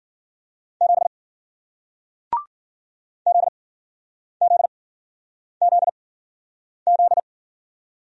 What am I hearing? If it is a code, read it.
Morse code, heard as a keyed tone, about 53 wpm, 700 Hz. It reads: 7 Z Z Z Z